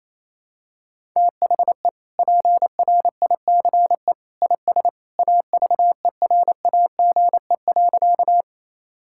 Morse code, heard as a keyed tone, about 28 wpm, 700 Hz. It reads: THE PRICE IS AVERAGE.
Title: Sentences from Top 1000 Words - Review Part 6 - 28wpm